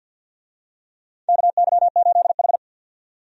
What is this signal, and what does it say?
Morse code, heard as a keyed tone, about 53 wpm, 700 Hz. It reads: KY8H
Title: USA Call Signs - 4 Characters - 50wpm